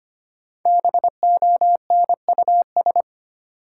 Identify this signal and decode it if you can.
Morse code, heard as a keyed tone, about 25 wpm, 700 Hz. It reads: BONUS
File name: 5 Letter Words - First 3 Letters Given - 25wpm